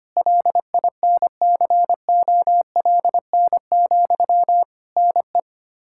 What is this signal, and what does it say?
Morse code, heard as a keyed tone, about 25 wpm, 700 Hz. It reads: LINCOLN, NE
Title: Practice QSO - Exchange 1 - Beginning - 25wpm